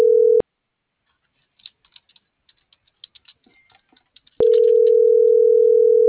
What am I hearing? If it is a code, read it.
Ringback tone (North America), 440 + 480 Hz.